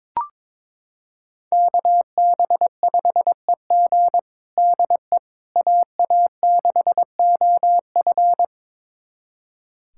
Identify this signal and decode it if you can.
Morse code, heard as a keyed tone, about 22 wpm, 700 Hz. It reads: KB5EG DE AA6OF
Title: USA Callsign DE Callsign - 5 Characters - 22wpm